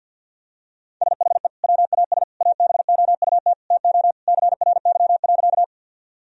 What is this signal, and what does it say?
Morse code, heard as a keyed tone, about 48 wpm, 700 Hz. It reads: SHE WAS ABOUT TO CRY.
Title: Sentences from Top 1000 Words - Review Part 2 - 50wpm